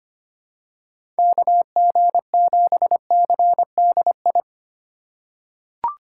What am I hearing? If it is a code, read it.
Morse code, heard as a keyed tone, about 25 wpm, 700 Hz. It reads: KG7CDI
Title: USA Call Signs - 6 Characters - 6x Character Spacing - 25wpm